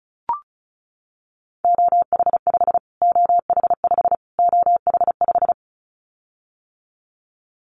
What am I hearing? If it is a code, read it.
Morse code, heard as a keyed tone, about 34 wpm, 700 Hz. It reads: OH5 OH5 OH5
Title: Callsign Prefixes for Top 25 DXCC - Repeated 3x - 35wpm